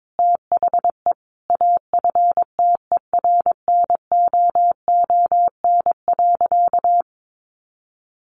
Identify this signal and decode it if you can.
Morse code, heard as a keyed tone, about 22 wpm, 700 Hz. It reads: THE AFTERNOON.
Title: Sentences from Top 1500 Words 22wpm